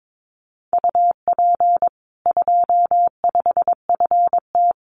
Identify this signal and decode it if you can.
Morse code, heard as a keyed tone, about 22 wpm, 700 Hz. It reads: UP 25FT
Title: Common QSO Phrases 22wpm